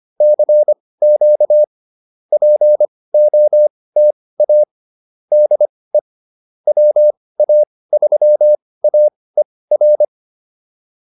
Morse code, heard as a keyed tone, about 25 wpm, spaced as for 18 wpm, 600 Hz. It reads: CQ POTA DE WA3AER